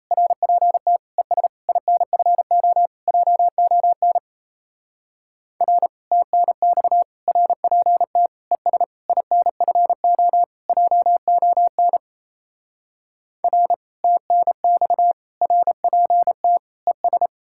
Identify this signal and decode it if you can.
Morse code, heard as a keyed tone, about 38 wpm, 700 Hz. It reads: RPT ES INFO JON R TNX RPT ES INFO JON R TNX RPT ES